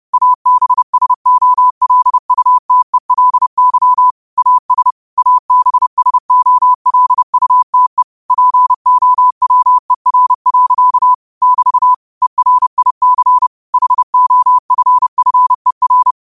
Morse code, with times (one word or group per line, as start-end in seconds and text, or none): 0.1-4.1 ABSOLUTELY
4.4-4.9 AS
5.2-8.0 ABSOLUTE
8.3-11.1 POWER.
11.4-11.9 <BT>
12.2-13.5 ERIC
13.7-16.1 HOFFER